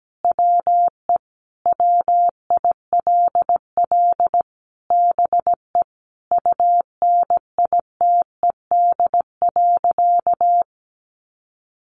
Morse code, heard as a keyed tone, about 17 wpm, 700 Hz. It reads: WE WILL BE UNITED.
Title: Sentences from Top 600 Words 17wpm